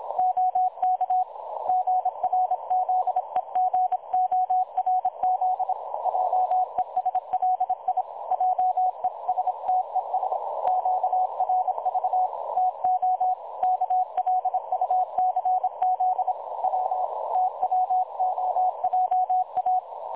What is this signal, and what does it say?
Morse code, heard as a keyed tone, about 26 wpm, 740 Hz. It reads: OK GRZEGORZ JESLI JESTES QRV TO KLUCZ W TWOJA
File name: dlazaawansowanych / nagraniazpasma / websdr recording 3565.0kHz